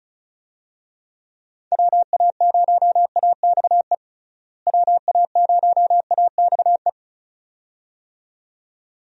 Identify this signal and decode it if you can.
Morse code, heard as a keyed tone, about 36 wpm, 700 Hz. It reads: WA0AXE WA0AXE
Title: USA Call Signs - 6 Characters - Repeated 2x - 35wpm